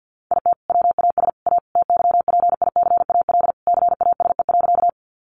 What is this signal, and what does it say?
Morse code, heard as a keyed tone, about 52 wpm, 700 Hz. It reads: IT WAS A TYPICAL CASE.
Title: Sentences from Top 1100 Words 50wpm